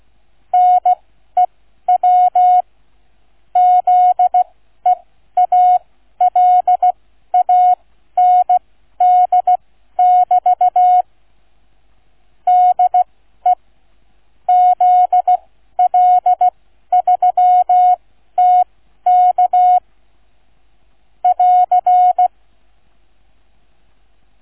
Morse code, heard as a keyed tone, about 16 wpm, spaced as for 12 wpm, 730 Hz. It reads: NEW ZEALAND<BT> DE ZL3TK <AR>
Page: QRS International Flight Contest 2025 Demo Flight NR 00 Sound files Listen-to or download incoming QNC sound file